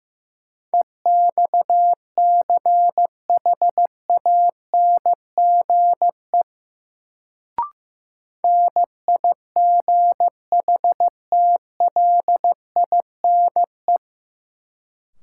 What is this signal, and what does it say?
Morse code, heard as a keyed tone, about 15 wpm, 700 Hz. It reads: EXCHANGE NIGHTLINE
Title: Sets of 5 Words - New Word Drill - 15wpm